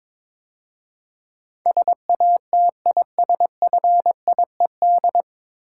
Morse code, heard as a keyed tone, about 22 wpm, 700 Hz. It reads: SATISFIED
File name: 9 Letter Words - First 3 Letters Given - 22wpm